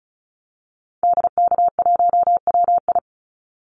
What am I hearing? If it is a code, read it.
Morse code, heard as a keyed tone, about 34 wpm, 700 Hz. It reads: DK1WI
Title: Call Signs Repeated 2x - 35wpm